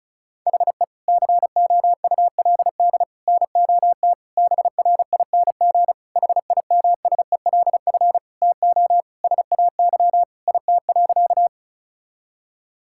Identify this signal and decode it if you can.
Morse code, heard as a keyed tone, about 34 wpm, 700 Hz. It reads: HE COULD NOT BRING HIMSELF TO SAY IT.
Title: Sentences from Top 1600 Words 35wpm